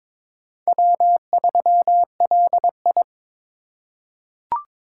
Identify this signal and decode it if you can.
Morse code, heard as a keyed tone, about 23 wpm, 700 Hz. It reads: W3LI
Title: USA Call Signs - 4 Characters - 3x Character Spacing - 22wpm